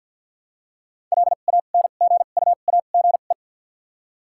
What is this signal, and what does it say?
Morse code, heard as a keyed tone, about 53 wpm, 700 Hz. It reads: L A N G U A G E
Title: 2x Character Spacing - Top 500 Words - 50wpm